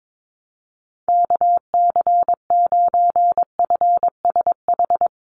Morse code, heard as a keyed tone, about 22 wpm, 700 Hz. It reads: KC9FSH
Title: USA Call Signs 22wpm